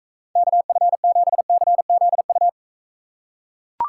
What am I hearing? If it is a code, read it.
Morse code, heard as a keyed tone, about 42 wpm, 700 Hz. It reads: KF7CZU